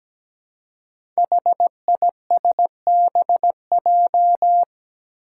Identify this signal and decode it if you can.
Morse code, heard as a keyed tone, about 17 wpm, 700 Hz. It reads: HISBJ
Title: Five Challenging Letter-Number Combinations 17wpm